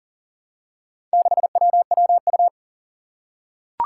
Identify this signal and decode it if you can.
Morse code, heard as a keyed tone, about 37 wpm, 700 Hz. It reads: 6WWU
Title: Four Challenging Letter-Number Combinations 40wpm